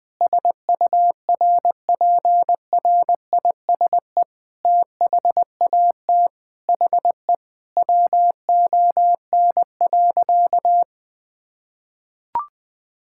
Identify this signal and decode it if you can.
Morse code, heard as a keyed tone, about 20 wpm, 700 Hz. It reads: SURPRISE THAT HE WON.
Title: Sentences with 7-Word Phrases - Spoken Beginning - 20wpm